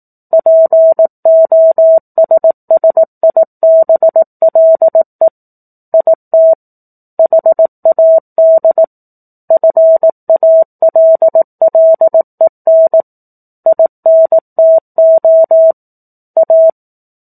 Morse code, heard as a keyed tone, about 18 wpm, 650 Hz. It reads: POSSIBLE IT HAD FALLEN INTO A
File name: 04-Chapter-04 18 WPM